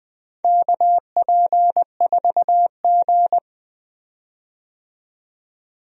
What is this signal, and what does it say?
Morse code, heard as a keyed tone, about 20 wpm, 700 Hz. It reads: KP4G
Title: DXCC-100 Call Signs - 4 Characters - 20wpm